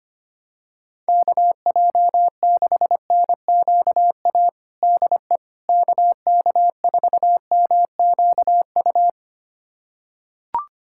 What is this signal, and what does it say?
Morse code, heard as a keyed tone, about 25 wpm, 700 Hz. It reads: KJ6NQA DE KK4MQU
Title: USA Callsign DE Callsign 25wpm